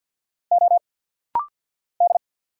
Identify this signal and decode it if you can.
Morse code, heard as a keyed tone, about 50 wpm, 690 Hz. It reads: O D
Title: Single Letter-Number Alt - Rapid-Fire 50wpm